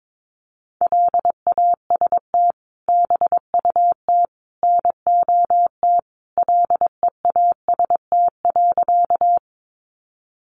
Morse code, heard as a keyed tone, about 22 wpm, 700 Hz. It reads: LAST BUT NOT LEAST.
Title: Sentences from Top 700 Words 22wpm